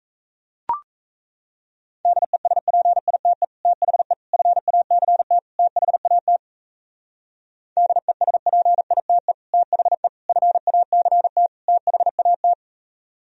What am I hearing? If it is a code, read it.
Morse code, heard as a keyed tone, about 36 wpm, 700 Hz. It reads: DESPITE THE FACT THAT DESPITE THE FACT THAT